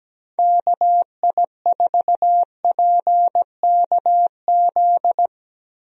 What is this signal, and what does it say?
Morse code, heard as a keyed tone, about 17 wpm, 700 Hz. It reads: KI4PKZ
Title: USA Call Signs - 6 Characters - Repeated 3x - 17wpm